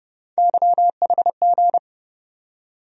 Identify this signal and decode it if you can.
Morse code, heard as a keyed tone, about 30 wpm, 700 Hz. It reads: YHG
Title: USA Call Sign Postfixes - 3 Letters 30wpm